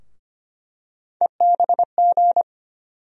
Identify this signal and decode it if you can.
Morse code, heard as a keyed tone, about 25 wpm, 700 Hz. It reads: EBG
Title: USA Call Sign Postfixes - 3 Letters 25wpm